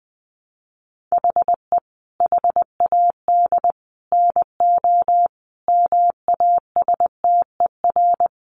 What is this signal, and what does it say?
Morse code, heard as a keyed tone, about 19 wpm, 700 Hz. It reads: HE HAD NO MASTER
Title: Sentences from Top 1300 Words 20wpm